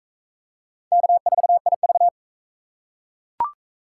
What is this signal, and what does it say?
Morse code, heard as a keyed tone, about 40 wpm, 700 Hz. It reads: K4IV